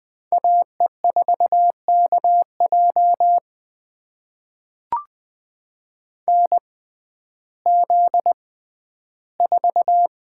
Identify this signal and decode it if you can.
Morse code, heard as a keyed tone, about 20 wpm, spaced as for 12 wpm, 700 Hz. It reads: AE4KJ NZ4
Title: USA Call Signs - 6x Character Spacing - 20wpm